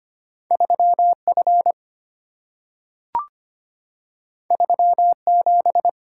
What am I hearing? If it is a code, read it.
Morse code, heard as a keyed tone, about 25 wpm, 700 Hz. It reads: 3F 37
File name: Two Letter-Number Combinations - 25wpm